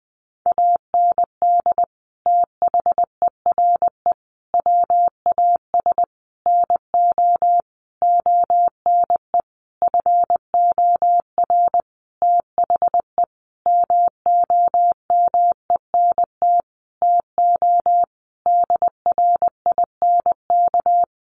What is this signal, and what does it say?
Morse code, heard as a keyed tone, about 20 wpm, 700 Hz. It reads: AND THERE WAS NO ONE FOR THE MOMENT TO DRINK
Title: Sentences from Top 1500 Words 20wpm